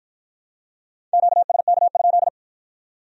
Morse code, heard as a keyed tone, about 51 wpm, 700 Hz. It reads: QSK?